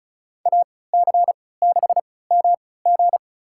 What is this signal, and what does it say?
Morse code, heard as a keyed tone, about 34 wpm, spaced as for 20 wpm, 700 Hz. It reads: AC6MG
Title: USA Call Signs - 5 Characters - 3x Character Spacing - 35wpm